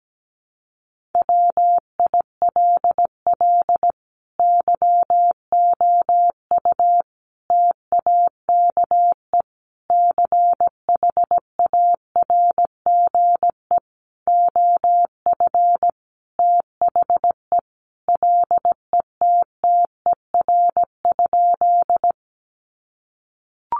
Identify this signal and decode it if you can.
Morse code, heard as a keyed tone, about 17 wpm, 700 Hz. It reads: WILL YOU TAKE CHARGE OF THE LETTER?
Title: Sentences from Top 900 Words 17wpm